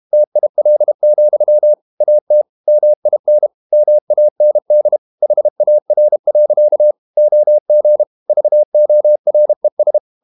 Morse code, heard as a keyed tone, about 32 wpm, 600 Hz. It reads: TIL, AT MIN MAND HAR. OG VORES
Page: Ferie 32 wpm